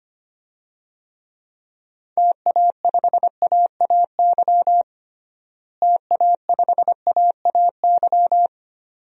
Morse code, heard as a keyed tone, about 25 wpm, 700 Hz. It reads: TA5AAY TA5AAY
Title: DXCC-100 Call Signs - 6 Characters - Repeated 2x - 25wpm